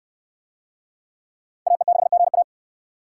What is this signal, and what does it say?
Morse code, heard as a keyed tone, about 69 wpm, 700 Hz. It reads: AE6GA